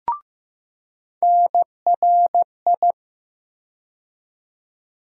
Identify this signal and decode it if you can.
Morse code, heard as a keyed tone, about 15 wpm, 700 Hz. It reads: NRI
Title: USA Call Sign Postfixes - 3 Letters 15wpm